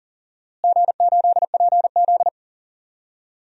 Morse code, heard as a keyed tone, about 40 wpm, 700 Hz. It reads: G8PZ
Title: DXCC-100 Call Signs - 4 Characters - Repeated 3x - 40wpm